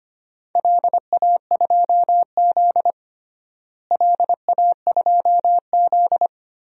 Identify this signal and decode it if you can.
Morse code, heard as a keyed tone, about 25 wpm, 700 Hz. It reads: LA2Z LA2Z
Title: DXCC-100 Call Signs - 4 Characters - Repeated 3x - 25wpm